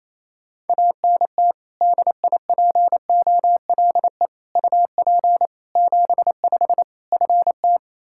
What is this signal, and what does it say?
Morse code, heard as a keyed tone, about 28 wpm, 700 Hz. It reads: ANT DIPOLE UP 75 FT
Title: Practice QSO - Exchange 2 - Info - 28wpm